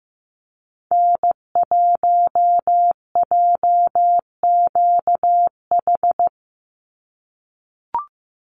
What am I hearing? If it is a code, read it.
Morse code, heard as a keyed tone, about 15 wpm, 700 Hz. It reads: N1JQH
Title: USA Call Signs 15wpm